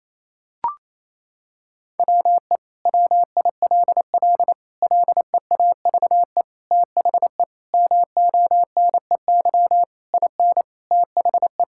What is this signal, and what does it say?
Morse code, heard as a keyed tone, about 28 wpm, 700 Hz. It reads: WE WILL LEAVE THE MONEY IN THE
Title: Sentences from Top 1000 Words - Review Part 6 - 28wpm